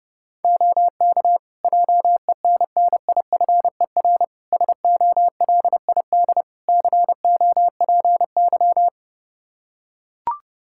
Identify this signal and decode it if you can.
Morse code, heard as a keyed tone, about 31 wpm, 700 Hz. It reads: OK JENNIFER SOLID COPY
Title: Practice QSO - Exchange 3 - More Info - 30wpm